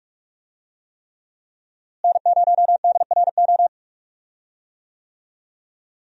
Morse code, heard as a keyed tone, about 46 wpm, 700 Hz. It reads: N0DRO
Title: USA Call Signs - 5 Characters - Repeated 3x - 45wpm